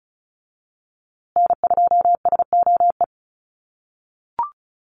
Morse code, heard as a keyed tone, about 34 wpm, 700 Hz. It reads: N2SOE
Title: USA Call Signs Repeated 2x 35wpm